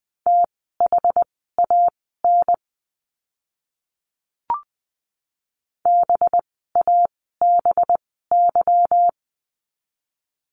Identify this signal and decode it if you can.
Morse code, heard as a keyed tone, about 20 wpm, spaced as for 14 wpm, 700 Hz. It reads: THAN BABY
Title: 185 Lesson 30f - Letter b - Words 20wpm